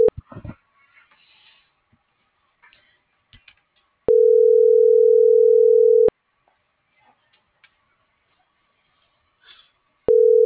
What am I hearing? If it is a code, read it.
Ringback tone (North America), 440 + 480 Hz.